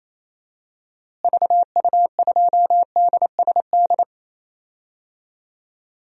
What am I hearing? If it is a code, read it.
Morse code, heard as a keyed tone, about 28 wpm, 700 Hz. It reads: VU2DSD
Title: DXCC-100 Call Signs - 6 Characters - Repeated 2x - 28wpm